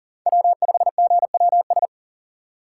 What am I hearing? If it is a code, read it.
Morse code, heard as a keyed tone, about 40 wpm, 700 Hz. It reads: W5GWS
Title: USA Call Signs - 5 Characters - 2x Character Spacing - 40wpm